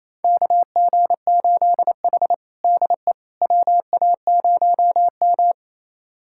Morse code, heard as a keyed tone, about 28 wpm, 700 Hz. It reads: KG8H DE WA0M
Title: USA Callsign DE Callsign - 4 Characters - 28wpm